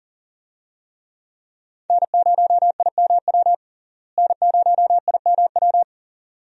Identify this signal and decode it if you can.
Morse code, heard as a keyed tone, about 37 wpm, 700 Hz. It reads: N0IMW N0IMW
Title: USA Call Signs - 5 Characters - Repeated 2x - 40wpm